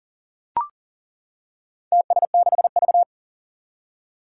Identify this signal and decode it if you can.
Morse code, heard as a keyed tone, about 40 wpm, 700 Hz. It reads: TS6V
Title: Four Challenging Letter-Number Combinations 40wpm